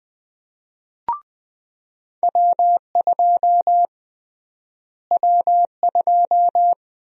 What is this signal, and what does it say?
Morse code, heard as a keyed tone, about 20 wpm, 700 Hz. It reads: W2 W2
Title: USA Call Sign Prefixes - Repeated 2x - 20wpm